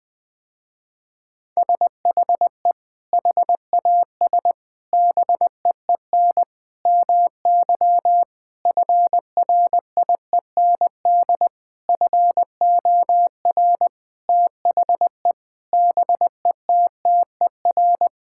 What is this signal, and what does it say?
Morse code, heard as a keyed tone, about 20 wpm, 700 Hz. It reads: SHE HAS BEEN MY FRIEND FOR THE BETTER
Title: Sentences with 6-Word Phrases - Spoken Beginning - 20wpm